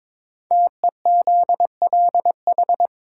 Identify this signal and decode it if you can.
Morse code, heard as a keyed tone, about 22 wpm, 700 Hz. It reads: TEZLH
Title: Code Groups - 5 Letters - 22wpm